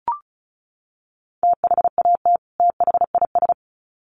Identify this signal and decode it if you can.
Morse code, heard as a keyed tone, about 34 wpm, 700 Hz. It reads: THAT THIS
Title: Sets of 2 Words 35wpm